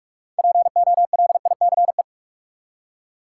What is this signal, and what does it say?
Morse code, heard as a keyed tone, about 43 wpm, 700 Hz. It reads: POLICE
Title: Six Letter Words - 45wpm